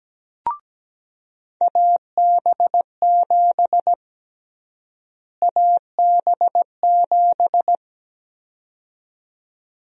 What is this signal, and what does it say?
Morse code, heard as a keyed tone, about 17 wpm, 700 Hz. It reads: AB7 AB7
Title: USA Call Sign Prefixes - Repeated 2x - 17wpm